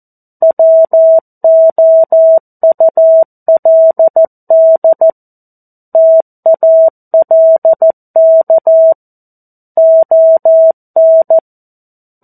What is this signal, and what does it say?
Morse code, heard as a keyed tone, about 14 wpm, 650 Hz. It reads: WOULD TALK ON
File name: 02-Chapter-02 14 WPM